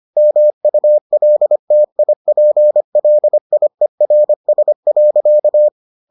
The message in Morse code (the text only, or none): MULTIPLIERS.